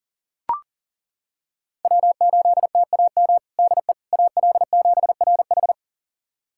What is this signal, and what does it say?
Morse code, heard as a keyed tone, about 40 wpm, 700 Hz. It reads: W8TAM DE AL7RH
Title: USA Callsign DE Callsign - 5 Characters - 40wpm